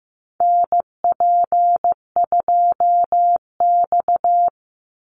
Morse code, heard as a keyed tone, about 15 wpm, 700 Hz. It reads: NP2X
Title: Call Signs Repeated 2x - 15wpm